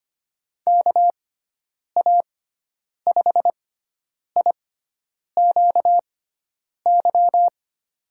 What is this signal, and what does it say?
Morse code, heard as a keyed tone, about 25 wpm, spaced as for 9 wpm, 700 Hz. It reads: KA5IQY
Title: USA Call Signs - 6 Characters - 6x Character Spacing - 25wpm